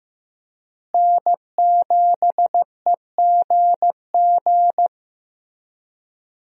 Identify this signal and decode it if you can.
Morse code, heard as a keyed tone, about 15 wpm, 700 Hz. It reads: N7EGG
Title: USA Call Signs - 5 Characters - Repeated 3x - 15wpm